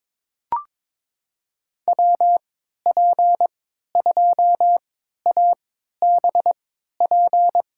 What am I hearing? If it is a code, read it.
Morse code, heard as a keyed tone, about 22 wpm, spaced as for 13 wpm, 700 Hz. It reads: WP2ABP
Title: USA Call Signs - 6 Characters - 3x Character Spacing - 22wpm